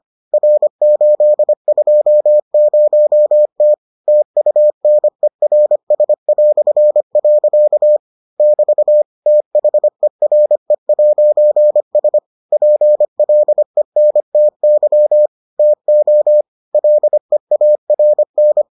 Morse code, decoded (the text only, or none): R820T TUNERS". <BT> THERE'S PLENTY TO LEARN